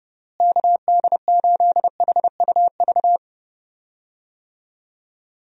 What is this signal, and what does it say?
Morse code, heard as a keyed tone, about 30 wpm, 700 Hz. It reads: KD8HUV
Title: USA Call Signs - 6 Characters - Repeated 2x - 30wpm